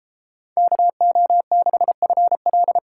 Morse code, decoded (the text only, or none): KO6FL